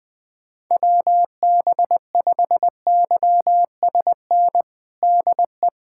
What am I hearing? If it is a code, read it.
Morse code, heard as a keyed tone, about 20 wpm, 700 Hz. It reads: WB5YSN DE
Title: USA Callsign DE Callsign - 6 Characters - 20wpm